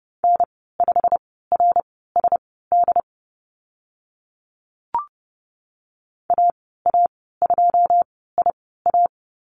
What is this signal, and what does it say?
Morse code, heard as a keyed tone, about 32 wpm, spaced as for 17 wpm, 690 Hz. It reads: N5RSD AA2IA
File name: Lesson 15g - Number 5 - Call Signs 30wpm